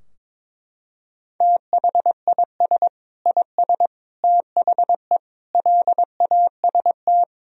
Morse code, heard as a keyed tone, about 22 wpm, 700 Hz. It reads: THIS IS THE LAST
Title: Sentences from Top 1000 Words - Review Part 4 - 22wpm